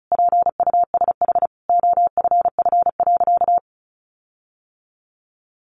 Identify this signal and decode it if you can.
Morse code, heard as a keyed tone, about 34 wpm, 700 Hz. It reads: PUSH OFF.
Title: Sentences from Top 1300 Words 35wpm